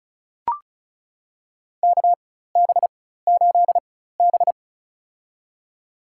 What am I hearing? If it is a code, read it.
Morse code, heard as a keyed tone, about 35 wpm, spaced as for 16 wpm, 700 Hz. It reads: KB8B
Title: USA Call Signs - 4 Characters - 4x Character Spacing - 35wpm